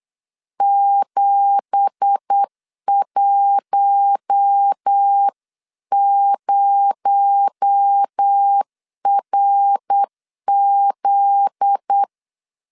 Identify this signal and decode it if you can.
Morse code, heard as a keyed tone, about 8 wpm, 800 Hz. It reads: 710RZ
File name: CW / DET / 1 9 1500C S C 8wpm 5653